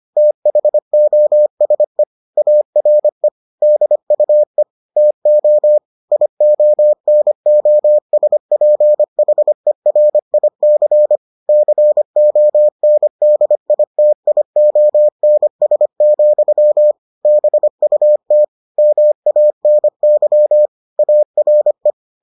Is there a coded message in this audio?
Yes: Morse code from 0.2 to 21.9 s, heard as a keyed tone, about 25 wpm, 600 Hz. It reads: THOSE ARE DUE TO IONOSPHERIC CONDITIONS, BUT MANY ARE